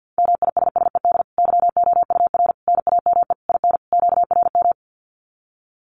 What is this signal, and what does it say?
Morse code, heard as a keyed tone, about 50 wpm, 700 Hz. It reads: MISSED YOUR NAME IN QRM
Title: Common QSO Phrases 50wpm